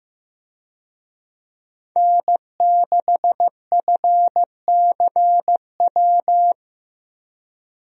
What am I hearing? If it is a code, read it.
Morse code, heard as a keyed tone, about 15 wpm, 700 Hz. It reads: N6FCW
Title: USA Call Signs - 5 Characters - Repeated 3x - 15wpm